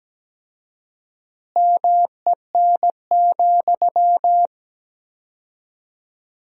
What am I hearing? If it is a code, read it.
Morse code, heard as a keyed tone, about 17 wpm, 700 Hz. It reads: MEN,
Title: 4x Word-Spacing - Sentences from Top 500 Words - 17wpm